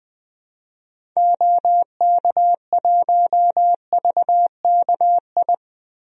Morse code, heard as a keyed tone, about 21 wpm, 700 Hz. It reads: OK1VKI
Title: DXCC-100 Call Signs - 6 Characters - Repeated 2x - 20wpm